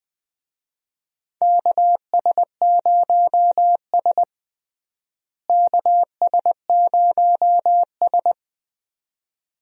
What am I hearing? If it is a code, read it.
Morse code, heard as a keyed tone, about 21 wpm, 700 Hz. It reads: KS0S KS0S
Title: USA Call Signs - 4 Characters - Repeated 2x - 20wpm